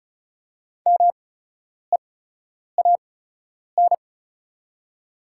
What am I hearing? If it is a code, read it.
Morse code, heard as a keyed tone, about 36 wpm, spaced as for 10 wpm, 700 Hz. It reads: MEAN
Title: 8x Character Spacing - Top 500 Words - 35wpm